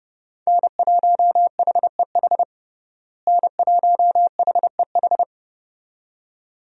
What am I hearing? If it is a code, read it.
Morse code, heard as a keyed tone, about 32 wpm, 700 Hz. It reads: N1HEH N1HEH
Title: USA Call Signs - 5 Characters - Repeated 2x - 30wpm